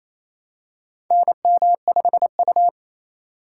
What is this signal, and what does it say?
Morse code, heard as a keyed tone, about 29 wpm, 700 Hz. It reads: NM5U
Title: USA Call Signs - 4 Characters - 3x Character Spacing - 28wpm